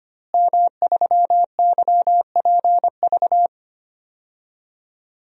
Morse code, heard as a keyed tone, about 25 wpm, 700 Hz. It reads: M3YPV
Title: DXCC-100 Call Signs - 5 Characters - Repeated 2x - 25wpm